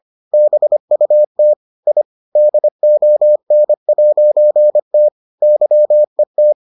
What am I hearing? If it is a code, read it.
Morse code, heard as a keyed tone, about 25 wpm, 600 Hz. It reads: BUT I DON'T YET